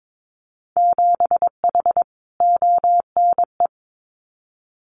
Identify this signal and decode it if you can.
Morse code, heard as a keyed tone, about 22 wpm, 700 Hz. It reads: 7H ONE
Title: Winter Field Day Exchanges Encore 22wpm